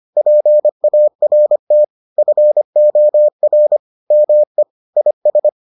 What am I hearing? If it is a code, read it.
Morse code, heard as a keyed tone, about 25 wpm, 600 Hz. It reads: PART FOR ME IS